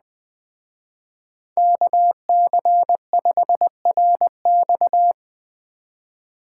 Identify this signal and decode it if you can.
Morse code, heard as a keyed tone, about 20 wpm, 700 Hz. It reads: KC5RX